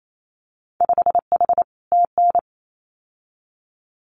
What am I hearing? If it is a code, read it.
Morse code, heard as a keyed tone, about 28 wpm, 700 Hz. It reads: 5H TN
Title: Winter Field Day Exchanges Encore 28wpm